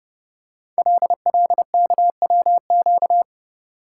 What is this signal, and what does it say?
Morse code, heard as a keyed tone, about 30 wpm, 700 Hz. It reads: LLKWQ
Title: Code Groups - 5 Letter:Numbers - 30wpm